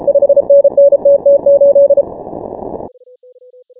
Morse code, heard as a keyed tone, about 33 wpm, 560 Hz. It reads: 5NN TT8
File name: final / W1UJ-168-140-MorseRunner